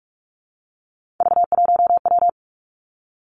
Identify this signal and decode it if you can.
Morse code, heard as a keyed tone, about 46 wpm, 700 Hz. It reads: V1W
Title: Three Challenging Letter-Number Combinations 45wpm